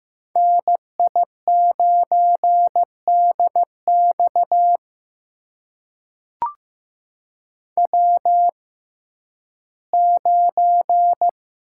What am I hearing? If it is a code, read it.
Morse code, heard as a keyed tone, about 15 wpm, spaced as for 11 wpm, 700 Hz. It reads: NI9DX W9
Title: USA Call Signs - 5 Characters - 6x Character Spacing - 15wpm